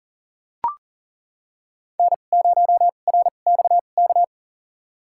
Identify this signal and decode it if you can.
Morse code, heard as a keyed tone, about 37 wpm, spaced as for 28 wpm, 700 Hz. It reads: N0RXK
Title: USA Call Signs - 5 Characters - 2x Character Spacing - 40wpm